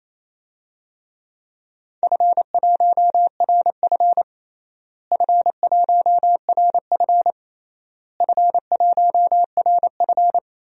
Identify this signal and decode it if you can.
Morse code, heard as a keyed tone, about 28 wpm, 700 Hz. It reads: F1RF F1RF F1RF
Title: DXCC-100 Call Signs - 4 Characters - Repeated 3x - 28wpm